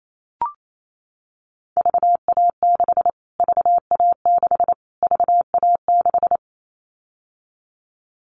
Morse code, heard as a keyed tone, about 28 wpm, 700 Hz. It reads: VA6 VA6 VA6
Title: Callsign Prefixes for Top 25 DXCC - Repeated 3x - 28wpm